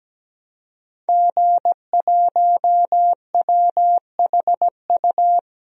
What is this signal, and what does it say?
Morse code, heard as a keyed tone, about 17 wpm, 700 Hz. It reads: G1WHU
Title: DXCC-100 Call Signs - 5 Characters - 17wpm